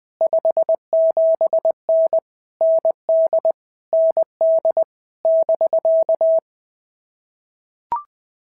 Morse code, heard as a keyed tone, about 19 wpm, 650 Hz. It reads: 57N ND ND <BK>
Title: POTA QSO Practice - 20wpm